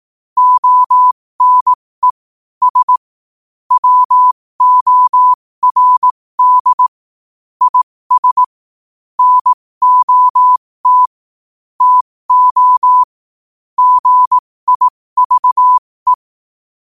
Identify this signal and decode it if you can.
Morse code, heard as a keyed tone, about 18 wpm, spaced as for 15 wpm, 1000 Hz. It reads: ONE S WORD IS NOT TO GIVE
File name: qotdfiles / 15wpm